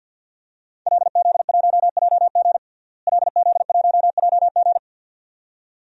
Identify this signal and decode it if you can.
Morse code, heard as a keyed tone, about 52 wpm, 700 Hz. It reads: LZ1JG LZ1JG
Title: DXCC-100 Call Signs - 5 Characters - Repeated 2x - 50wpm